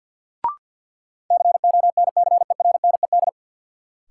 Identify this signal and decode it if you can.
Morse code, heard as a keyed tone, about 49 wpm, 700 Hz. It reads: CONCERNED
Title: Sentences from Top 1000 Words - Review Part 6 - New Word Drill - 50wpm